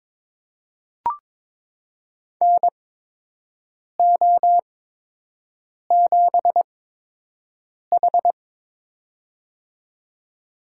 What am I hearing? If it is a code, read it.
Morse code, heard as a keyed tone, about 22 wpm, spaced as for 6 wpm, 700 Hz. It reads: NO7H
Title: USA Call Signs - 4 Characters - 8x Character Spacing - 22wpm